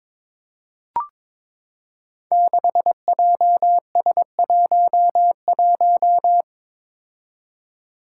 Morse code, heard as a keyed tone, about 22 wpm, 700 Hz. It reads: 6JS11
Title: Five Challenging Letter-Number Combinations 22wpm